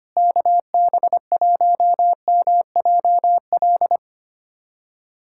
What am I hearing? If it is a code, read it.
Morse code, heard as a keyed tone, about 25 wpm, 700 Hz. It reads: KB1MJL
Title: USA Callsign DE Callsign - 6 Characters - 25wpm